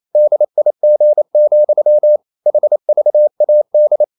Morse code, heard as a keyed tone, about 28 wpm, 600 Hz. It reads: DIG, HVAD
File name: Ferie QRQ 28wpm